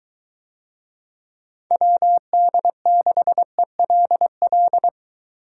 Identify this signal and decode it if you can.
Morse code, heard as a keyed tone, about 21 wpm, 700 Hz. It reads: WD6ELL